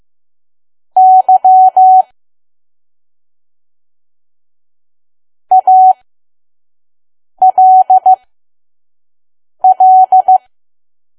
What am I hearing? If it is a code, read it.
Morse code, heard as a keyed tone, about 15 wpm, spaced as for 5 wpm, 750 Hz. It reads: Y ALL
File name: Archive / 5 WPM files / PM